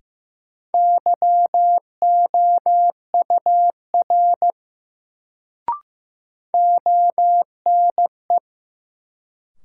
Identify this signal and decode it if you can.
Morse code, heard as a keyed tone, about 15 wpm, 700 Hz. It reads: YOUR ONE
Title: Sets of 2 Words - New Word Drill - 15wpm